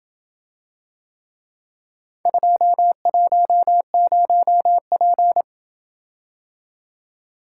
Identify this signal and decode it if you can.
Morse code, heard as a keyed tone, about 27 wpm, 700 Hz. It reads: 210P